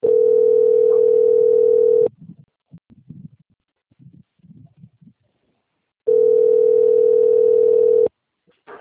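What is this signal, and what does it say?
Ringback tone (North America), 440 + 480 Hz.
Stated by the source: Outgoing telephone call recorded at Ribeirao Preto, SP.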